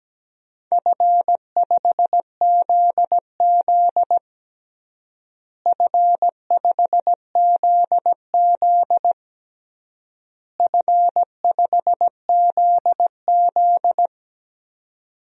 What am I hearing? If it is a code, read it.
Morse code, heard as a keyed tone, about 17 wpm, 700 Hz. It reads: F5ZZ F5ZZ F5ZZ
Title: DXCC-100 Call Signs - 4 Characters - Repeated 3x - 17wpm